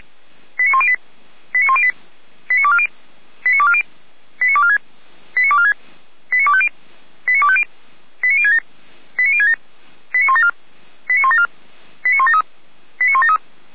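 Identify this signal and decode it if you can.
Five-tone selective call (ZVEI), heard as quick runs of five tones, tones of about 70 ms. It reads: 89108 89108 89240 89240 89246 89246 89250 89250 89067 89067 89174 89174 89173 89173
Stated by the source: ZVEI five-tone call received by SDR, from feuerwehr-orlamuende (fire brigade Orlamünde).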